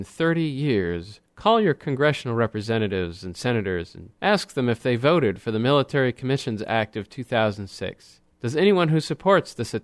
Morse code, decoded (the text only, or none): none